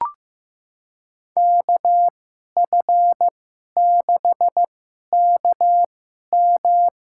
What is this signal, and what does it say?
Morse code, heard as a keyed tone, about 15 wpm, spaced as for 11 wpm, 700 Hz. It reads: KF6KM
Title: USA Call Signs - 2x Character Spacing - 15wpm